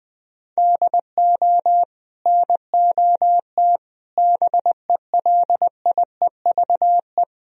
Morse code, heard as a keyed tone, about 20 wpm, 700 Hz. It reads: DO NOT BELIEVE
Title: Sentences from Top 1000 Words - Review Part 4 - 20wpm